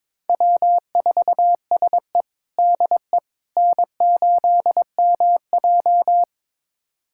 Morse code, heard as a keyed tone, about 22 wpm, 700 Hz. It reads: W4SE DE N8MJ
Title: USA Callsign DE Callsign - 4 Characters - 22wpm